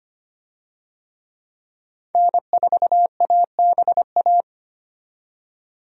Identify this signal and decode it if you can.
Morse code, heard as a keyed tone, about 25 wpm, 690 Hz. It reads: N4ABA
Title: USA Call Signs - 5 Characters - 25wpm